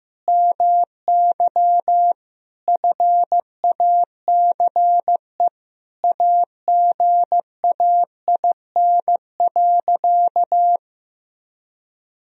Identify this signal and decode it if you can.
Morse code, heard as a keyed tone, about 15 wpm, 700 Hz. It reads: MY FACE AGAIN.
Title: Sentences from Top 1000 Words - Review Part 2 - 15wpm